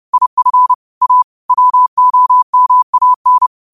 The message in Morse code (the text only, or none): IF A WOMAN